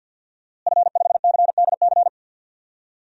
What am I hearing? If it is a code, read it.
Morse code, heard as a keyed tone, about 52 wpm, 700 Hz. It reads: F5CDC
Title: DXCC-100 Call Signs - 5 Characters - Repeated 2x - 50wpm